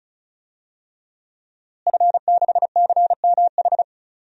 Morse code, heard as a keyed tone, about 36 wpm, 700 Hz. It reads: F6CMH
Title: DXCC-100 Call Signs - 5 Characters - Repeated 3x - 35wpm